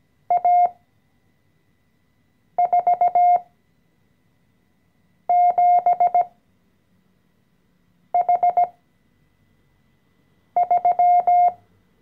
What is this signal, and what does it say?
Morse code, heard as a keyed tone, about 17 wpm, spaced as for 4 wpm, 710 Hz. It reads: A47H3
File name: CW-Online-Kurs / Anfänger / Lektion 014